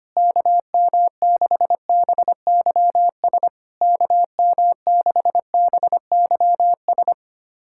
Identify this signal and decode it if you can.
Morse code, heard as a keyed tone, about 25 wpm, 690 Hz. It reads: KM6BYS KM6BYS
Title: SOTA QSO Practice 25wpm